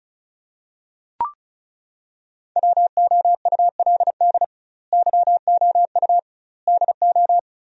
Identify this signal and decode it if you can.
Morse code, heard as a keyed tone, about 36 wpm, 700 Hz. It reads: WOULD YOU DO
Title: 2x Word-Spacing - Sentences from Top 100 Words - 35wpm